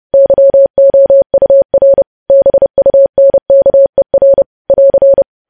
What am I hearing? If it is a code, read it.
Morse code, heard as a keyed tone, about 30 wpm, 550 Hz. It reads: YOUR BUNKER <AR>
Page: news headlines in morse code at 30 wpm